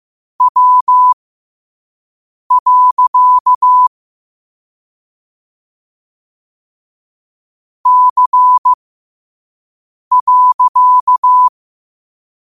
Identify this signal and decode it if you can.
Morse code, heard as a keyed tone, about 15 wpm, spaced as for 5 wpm, 1000 Hz. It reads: W. C.
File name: qotdfiles / 5wpm